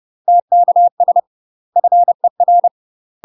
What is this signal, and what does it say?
Morse code, heard as a keyed tone, about 30 wpm, 700 Hz. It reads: TKS FER
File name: UK MASTER QSOs TRAINING / 30WPM / UK MASTER QSOs 0003